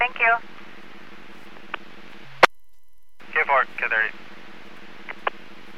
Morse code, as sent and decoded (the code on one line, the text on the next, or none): none